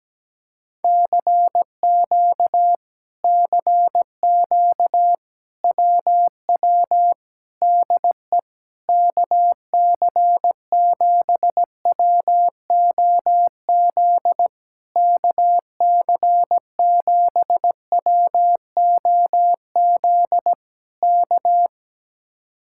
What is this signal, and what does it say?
Morse code, heard as a keyed tone, about 17 wpm, 700 Hz. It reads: CQ CQ WW DE KC7WOZ KC7WOZ K
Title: Speed-Racing Common QSO Phrases - 17wpm